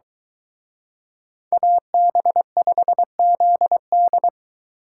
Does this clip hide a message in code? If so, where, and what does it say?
Yes: Morse code from 1.5 to 4.3 s, heard as a keyed tone, about 23 wpm, 700 Hz. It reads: AB5ZD